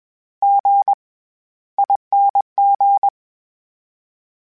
Morse code, heard as a keyed tone, about 21 wpm, spaced as for 14 wpm, 800 Hz. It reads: GING
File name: Special INGs-mp3s / Special INGs-mp3s / SlowToFaster ing @20